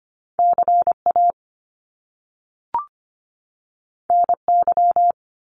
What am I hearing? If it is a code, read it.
Morse code, heard as a keyed tone, about 25 wpm, 700 Hz. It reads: CA NY
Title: US States 25wpm